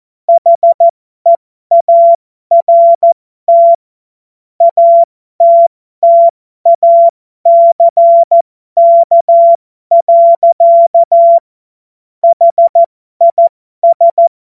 Morse code, heard as a keyed tone, about 14 wpm, spaced as for 12 wpm, 680 Hz. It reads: HEART ATTACK. HIS
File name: Morsekurs / Witze